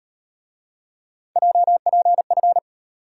Morse code, heard as a keyed tone, about 38 wpm, 700 Hz. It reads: JPF